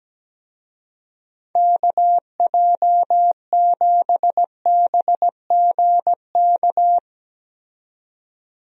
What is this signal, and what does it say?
Morse code, heard as a keyed tone, about 17 wpm, 700 Hz. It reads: KJ7BGK
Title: USA Call Signs - 6 Characters - 17wpm